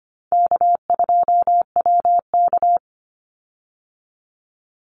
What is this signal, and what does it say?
Morse code, heard as a keyed tone, about 25 wpm, 700 Hz. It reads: K2WK
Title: Easy Call Signs Repeated 3x - 25wpm